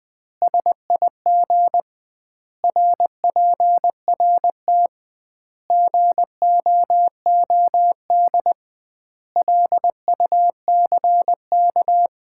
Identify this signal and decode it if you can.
Morse code, heard as a keyed tone, about 20 wpm, 700 Hz. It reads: SIG RPRT GOOD LUCK
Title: Common QSO Phrases - 2x Word-Spacing - 20wpm